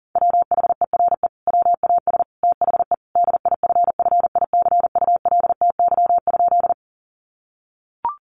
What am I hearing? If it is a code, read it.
Morse code, heard as a keyed tone, about 40 wpm, 700 Hz. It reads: WHERE WAS THE DIFFICULTY?
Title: Sentences from Top 1500 Words 40wpm